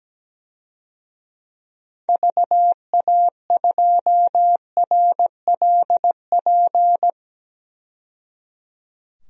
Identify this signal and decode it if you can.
Morse code, heard as a keyed tone, about 17 wpm, 700 Hz. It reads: VA2RLP
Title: DXCC-100 Call Signs - 6 Characters - 17wpm